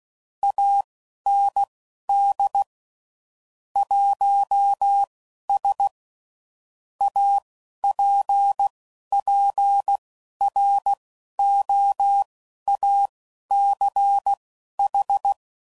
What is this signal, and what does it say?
Morse code, heard as a keyed tone, about 16 wpm, spaced as for 11 wpm, 800 Hz. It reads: AND 1S APPROACH
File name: CW Code Groups @10WPM / Technical Subjects / Gates Intro